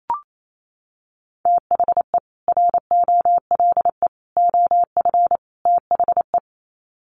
Morse code, heard as a keyed tone, about 28 wpm, 700 Hz. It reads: THE ROLE OF THE
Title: Sets of 4 Words 28wpm